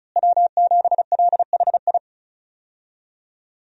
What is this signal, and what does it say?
Morse code, heard as a keyed tone, about 34 wpm, 700 Hz. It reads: W7LHI
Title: USA Call Signs - 5 Characters - Repeated 3x - 35wpm